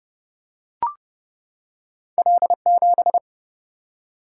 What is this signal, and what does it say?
Morse code, heard as a keyed tone, about 30 wpm, 700 Hz. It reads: L7
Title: Single-Letter Single-Number 30wpm